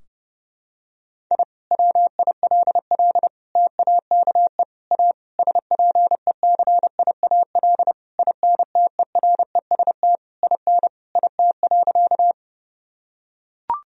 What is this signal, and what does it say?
Morse code, heard as a keyed tone, about 30 wpm, 700 Hz. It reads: I WILL TAKE A SPECIAL INTEREST IN IT.
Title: Sentences from Top 1000 Words - Review Part 3 - 30wpm